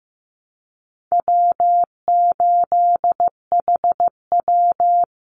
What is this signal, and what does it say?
Morse code, heard as a keyed tone, about 15 wpm, 700 Hz. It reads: W8HW
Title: Easy Call Signs Repeated 3x - 15wpm